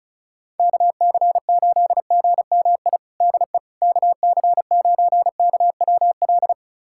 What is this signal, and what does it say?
Morse code, heard as a keyed tone, about 36 wpm, 700 Hz. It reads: KC8GMI DE KC9KWL
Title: USA Callsign DE Callsign - 6 Characters - 35wpm